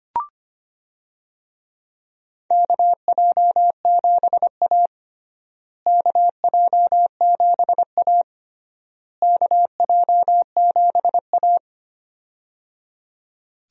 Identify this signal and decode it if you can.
Morse code, heard as a keyed tone, about 25 wpm, 700 Hz. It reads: KJ7A KJ7A KJ7A
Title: USA Call Signs - 4 Characters - Repeated 3x - 25wpm